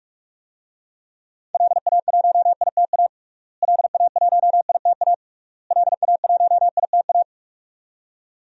Morse code, heard as a keyed tone, about 46 wpm, 700 Hz. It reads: LA1ITA LA1ITA LA1ITA
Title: DXCC-100 Call Signs - 6 Characters - Repeated 3x - 45wpm